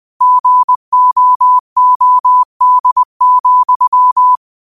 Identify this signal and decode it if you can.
Morse code, heard as a keyed tone, about 20 wpm, 1000 Hz. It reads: GOOD,
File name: qotdfiles / 20wpm